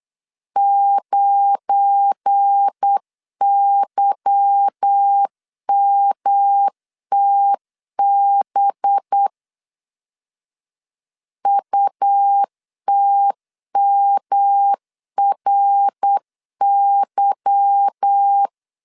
Morse code, heard as a keyed tone, about 8 wpm, 800 Hz. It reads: 9YMTB UTMRY
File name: LEZIONI / 8 YJ45 / 8 Y 8wpm 3245